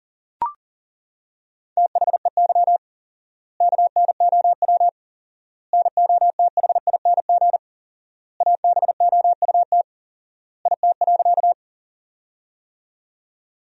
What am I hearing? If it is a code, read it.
Morse code, heard as a keyed tone, about 40 wpm, 700 Hz. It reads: THEY KNOW NOTHING ABOUT IT.
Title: 4x Word-Spacing - Sentences from Top 500 Words - 40wpm